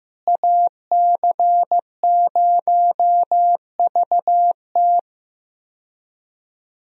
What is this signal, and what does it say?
Morse code, heard as a keyed tone, about 15 wpm, 700 Hz. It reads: AC0VT
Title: USA Call Signs - 5 Characters - Repeated 2x - 15wpm